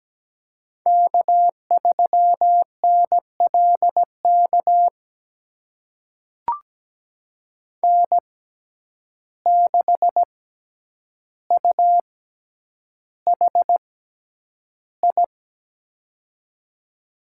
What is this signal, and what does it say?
Morse code, heard as a keyed tone, about 17 wpm, spaced as for 9 wpm, 700 Hz. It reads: K3NLK N6UHI
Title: USA Call Signs - 6x Character Spacing - 17wpm